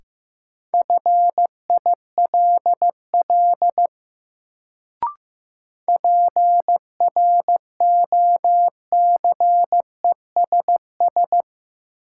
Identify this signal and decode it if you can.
Morse code, heard as a keyed tone, about 15 wpm, 700 Hz. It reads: FILL PROCESS
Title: Sentences from Top 1000 Words - Review Part 3 - New Word Drill - 15wpm